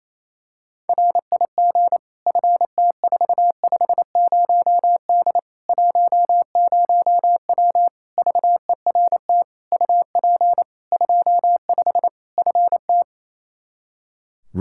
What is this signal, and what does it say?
Morse code, heard as a keyed tone, about 28 wpm, 700 Hz. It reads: RIG FT450D 10W VERT UP 25 FT
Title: Practice QSO - Exchange 2 - Info - 28wpm